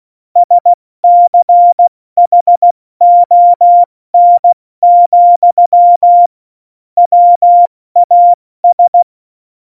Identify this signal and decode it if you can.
Morse code, heard as a keyed tone, about 16 wpm, 700 Hz. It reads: SCHON, WAS
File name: Morsekurs / Witze / Witz 29. Januar 2019, 70 F80